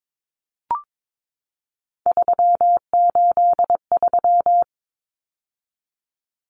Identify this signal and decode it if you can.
Morse code, heard as a keyed tone, about 22 wpm, 700 Hz. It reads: 383
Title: Numbers - 3-Digit 22wpm